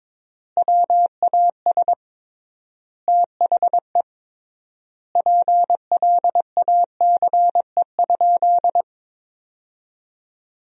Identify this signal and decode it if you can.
Morse code, heard as a keyed tone, about 22 wpm, 700 Hz. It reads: WAS THE PLACE?
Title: 3x Word-Spacing - Sentences from Top 200 Words - 22wpm